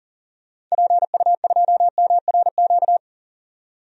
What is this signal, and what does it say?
Morse code, heard as a keyed tone, about 38 wpm, 700 Hz. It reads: PU2MRQ
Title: DXCC-100 Call Signs - 6 Characters - 40wpm